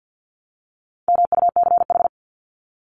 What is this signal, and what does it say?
Morse code, heard as a keyed tone, about 48 wpm, 700 Hz. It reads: MUCH
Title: Top 300 Words 50wpm